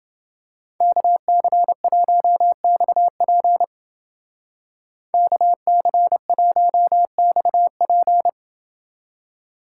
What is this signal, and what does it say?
Morse code, heard as a keyed tone, about 29 wpm, 700 Hz. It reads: KC1XP KC1XP